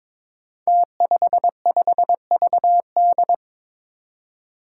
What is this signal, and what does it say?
Morse code, heard as a keyed tone, about 21 wpm, 700 Hz. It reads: T55VD
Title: Five Challenging Letter-Number Combinations 22wpm